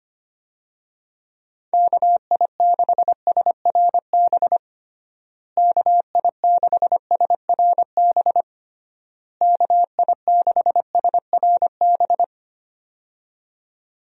Morse code, heard as a keyed tone, about 25 wpm, 700 Hz. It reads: KI6SRB KI6SRB KI6SRB
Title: USA Call Signs - 6 Characters - Repeated 3x - 25wpm